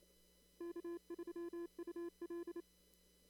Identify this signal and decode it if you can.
Morse code, heard as a keyed tone, about 28 wpm, 340 Hz. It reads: K3UL
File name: CW / kt5j wpx cw-op-k5pi